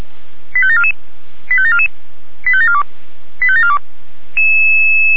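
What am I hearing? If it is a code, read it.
Five-tone selective call (ZVEI), heard as quick runs of five tones, tones of about 70 ms. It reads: 85744 85744 85742 85742
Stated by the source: ZVEI five-tone call received on SDR, from the Feuerwehr Orlamünde fire brigade audio server.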